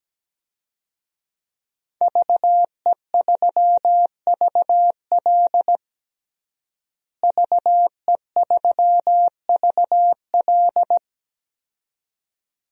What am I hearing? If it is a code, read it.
Morse code, heard as a keyed tone, about 17 wpm, 700 Hz. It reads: VE3VL VE3VL
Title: DXCC-100 Call Signs - 5 Characters - Repeated 2x - 17wpm